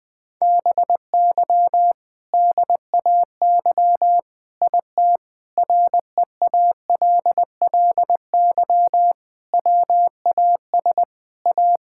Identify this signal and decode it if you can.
Morse code, heard as a keyed tone, about 21 wpm, 700 Hz. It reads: BY DAY IT REALLY WAS A
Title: Sentences from Top 1600 Words 20wpm